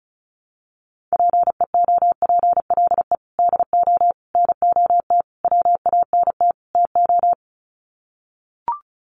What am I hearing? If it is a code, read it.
Morse code, heard as a keyed tone, about 34 wpm, 700 Hz. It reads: PEOPLE DO NOT WANT TO
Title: Sets of 5 Words Encore2 35wpm